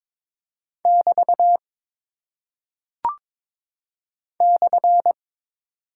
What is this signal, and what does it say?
Morse code, heard as a keyed tone, about 23 wpm, 700 Hz. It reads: <BT> /
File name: practice / 22wpm / Single Punctuation-Prosign - 22wpm